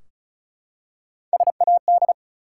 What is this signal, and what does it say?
Morse code, heard as a keyed tone, about 36 wpm, 700 Hz. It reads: SAD
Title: Three Letter Words - 35wpm